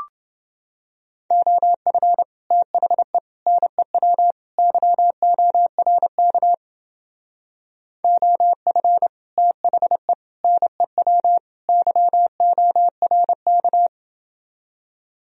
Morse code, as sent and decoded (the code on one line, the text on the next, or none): --- ..-. / - .... / -. .-- / -.-- --- .-. -.- / --- ..-. / - .... / -. .-- / -.-- --- .-. -.-
OF THE NEW YORK OF THE NEW YORK